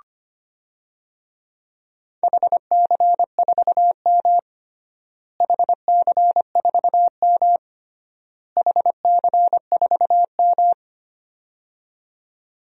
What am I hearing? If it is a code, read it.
Morse code, heard as a keyed tone, about 25 wpm, 700 Hz. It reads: HC4M HC4M HC4M
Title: DXCC-100 Call Signs - 4 Characters - Repeated 3x - 25wpm